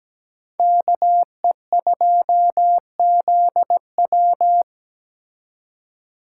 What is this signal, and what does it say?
Morse code, heard as a keyed tone, about 20 wpm, 700 Hz. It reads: KE2ZW
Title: Speed Racing - USA 2x2 Call Signs 17wpm